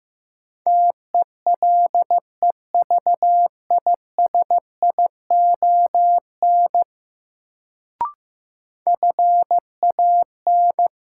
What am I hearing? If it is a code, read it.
Morse code, heard as a keyed tone, about 15 wpm, 700 Hz. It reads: TELEVISION FAN
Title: Sentences from Top 1000 Words - Review Part 4 - New Word Drill - 15wpm